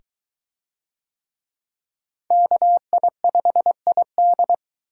Morse code, heard as a keyed tone, about 23 wpm, 700 Hz. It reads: KI5ID